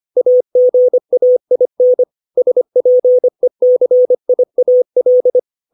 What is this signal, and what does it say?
Morse code, heard as a keyed tone, about 25 wpm, 500 Hz. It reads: AGAIN SPECIAL
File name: 632 words / 632 Words 25WPM